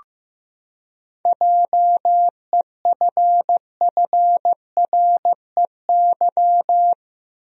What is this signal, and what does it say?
Morse code, heard as a keyed tone, about 15 wpm, 700 Hz. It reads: JEFFREY
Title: USA Common Names - 15wpm